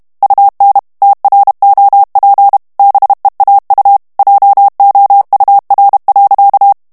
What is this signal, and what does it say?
Morse code, heard as a keyed tone, about 32 wpm, 800 Hz. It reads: UN TROP BEAU JOUR.
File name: morse mp3 / FRANCAIS TEXTE CLAIR 1 VITESSE 30 MOTS MINUTES